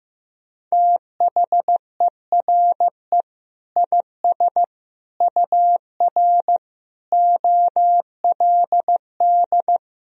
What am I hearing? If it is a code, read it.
Morse code, heard as a keyed tone, about 15 wpm, 700 Hz. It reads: THERE IS UR OLD
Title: Sentences from Top 500 Words and Abbreviations 15wpm